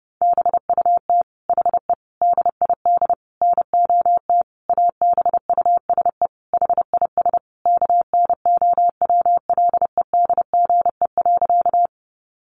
Morse code, heard as a keyed tone, about 30 wpm, 700 Hz. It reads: BUT HE DID NOT ABUSE HIS KNOWLEDGE.
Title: Sentences from Top 1300 Words 30wpm